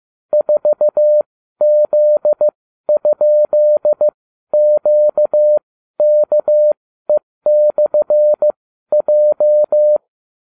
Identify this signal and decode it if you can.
Morse code, heard as a keyed tone, about 15 wpm, spaced as for 12 wpm, 600 Hz. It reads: 4Z?QKE/J